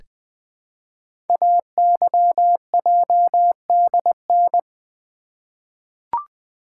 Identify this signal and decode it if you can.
Morse code, heard as a keyed tone, about 20 wpm, 700 Hz. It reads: AYJDN
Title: Code Groups - 5 Letters - 20wpm